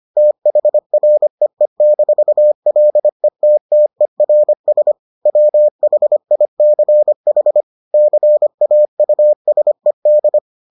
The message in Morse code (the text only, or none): THREE-LETTERS WHICH CAUSED